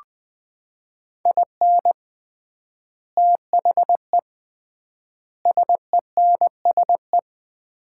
Morse code, heard as a keyed tone, about 21 wpm, 700 Hz. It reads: IN THE SENSE
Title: 3x Word-Spacing - Sets of 4 Words - 20wpm